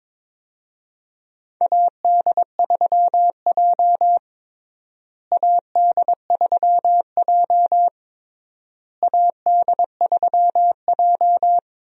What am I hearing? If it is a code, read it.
Morse code, heard as a keyed tone, about 23 wpm, 700 Hz. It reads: AD3J AD3J AD3J
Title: USA Call Signs - 4 Characters - Repeated 3x - 22wpm